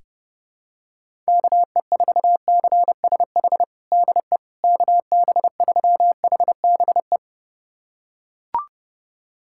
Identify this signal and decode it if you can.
Morse code, heard as a keyed tone, about 30 wpm, 700 Hz. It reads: KE4CSH DE KB3HBE
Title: USA Callsign DE Callsign 30wpm